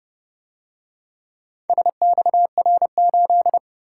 Morse code, heard as a keyed tone, about 29 wpm, 700 Hz. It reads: SXR8